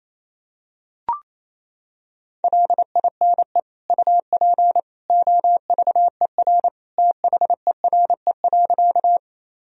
Morse code, heard as a keyed tone, about 28 wpm, 700 Hz. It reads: LINE UP OVER THERE.
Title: Sentences from Top 1000 Words - Review Part 1 - 28wpm